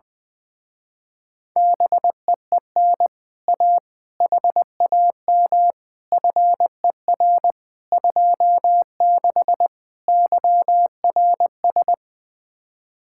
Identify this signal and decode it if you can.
Morse code, heard as a keyed tone, about 20 wpm, 700 Hz. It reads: BEEN A HAM FER 26 YRS
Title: Speed-Racing Common QSO Phrases - 20wpm